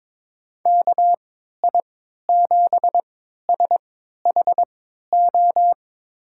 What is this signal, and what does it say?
Morse code, heard as a keyed tone, about 22 wpm, spaced as for 13 wpm, 700 Hz. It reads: KI7SHO
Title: USA Call Signs - 6 Characters - 3x Character Spacing - 22wpm